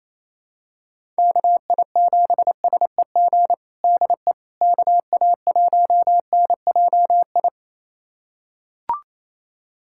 Morse code, heard as a keyed tone, about 28 wpm, 700 Hz. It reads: KI7SEG DE KA1NJI
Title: USA Callsign DE Callsign - 6 Characters - 28wpm